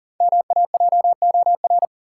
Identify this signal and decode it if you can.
Morse code, heard as a keyed tone, about 40 wpm, 700 Hz. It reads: MAJOR
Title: Sentences from Top 600 Words - New Word Drill - 40wpm